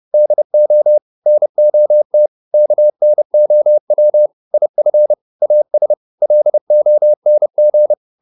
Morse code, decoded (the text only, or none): DO NOT KNOW IF AS LONG